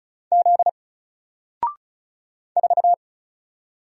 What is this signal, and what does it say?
Morse code, heard as a keyed tone, about 33 wpm, 700 Hz. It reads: Z 4
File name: Single-Letter-Number-35wpm